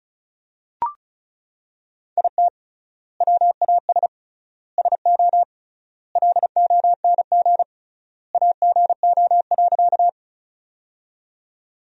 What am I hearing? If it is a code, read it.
Morse code, heard as a keyed tone, about 34 wpm, 700 Hz. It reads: IT WAS SO LONG AGO.
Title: 3x Word-Spacing - Sentences from Top 500 Words - 35wpm